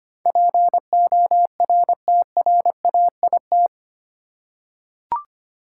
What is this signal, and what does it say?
Morse code, heard as a keyed tone, about 25 wpm, 700 Hz. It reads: PORTRAIT
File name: Eight Letter Words - 25wpm